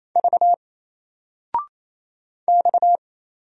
Morse code, heard as a keyed tone, about 28 wpm, 700 Hz. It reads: V X
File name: Single-Letter-Number-28wpm